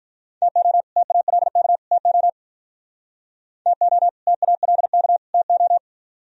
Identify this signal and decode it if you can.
Morse code, heard as a keyed tone, about 45 wpm, 700 Hz. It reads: TO TALK TO TO TALK TO